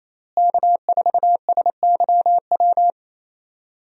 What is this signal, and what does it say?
Morse code, heard as a keyed tone, about 27 wpm, 700 Hz. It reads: K4SYW
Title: USA Call Signs - 5 Characters - Repeated 3x - 28wpm